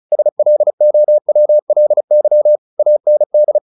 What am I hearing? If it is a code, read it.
Morse code, heard as a keyed tone, about 34 wpm, 600 Hz. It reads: SLOWLY AND
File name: the old man and the sea / the old man and the sea 1 35wpm